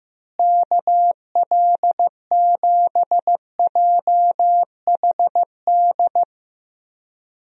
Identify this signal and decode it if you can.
Morse code, heard as a keyed tone, about 14 wpm, 700 Hz. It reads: KL7JHD
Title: USA Call Signs - 6 Characters - Repeated 3x - 15wpm